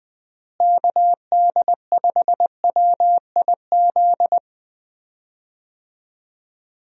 Morse code, heard as a keyed tone, about 21 wpm, 700 Hz. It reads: KD5WIZ
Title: USA Call Signs - 6 Characters - Repeated 2x - 20wpm